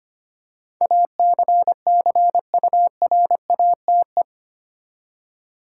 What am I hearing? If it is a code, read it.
Morse code, heard as a keyed tone, about 25 wpm, 700 Hz. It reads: ACCURATE
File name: Eight Letter Words - 25wpm